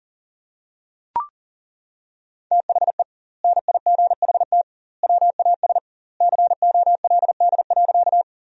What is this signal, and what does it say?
Morse code, heard as a keyed tone, about 40 wpm, 700 Hz. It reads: THE NIGHT WAS COLD.
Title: 2x Word-Spacing - Sentences from Top 400 Words - 40wpm